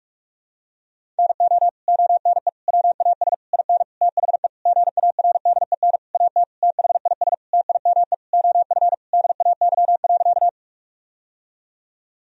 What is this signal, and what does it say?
Morse code, heard as a keyed tone, about 46 wpm, 700 Hz. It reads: NO ONE WAS IN THE GARDEN AT THIS TIME OF DAY.
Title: Sentences from Top 1000 Words - Review Part 4 - 45wpm